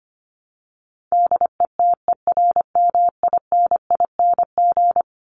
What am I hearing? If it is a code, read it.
Morse code, heard as a keyed tone, about 25 wpm, 700 Hz. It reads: DETERMINING
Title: ING Suffix - Top 100 Words with Base Words 25wpm